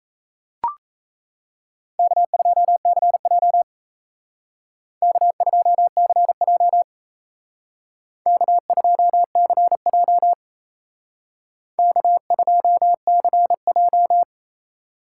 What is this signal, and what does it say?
Morse code, heard as a keyed tone, about 32 wpm, 700 Hz. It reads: K2CJ K2CJ K2CJ K2CJ
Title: Speed Racing - USA 1x2 Call Signs 28wpm